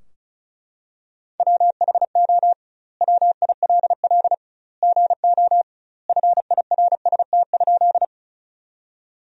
Morse code, heard as a keyed tone, about 36 wpm, 700 Hz. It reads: WHO WILL GO FIRST?
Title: 2x Word-Spacing - Sentences from Top 100 Words - 35wpm